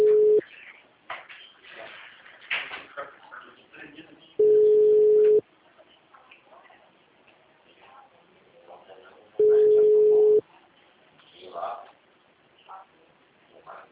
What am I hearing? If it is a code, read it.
Ringback tone (Europe, 425 Hz), with speech.